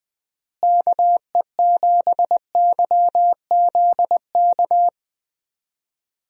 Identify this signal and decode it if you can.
Morse code, heard as a keyed tone, about 21 wpm, 700 Hz. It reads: KE7YZK
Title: USA Call Signs - 6 Characters - Repeated 2x - 20wpm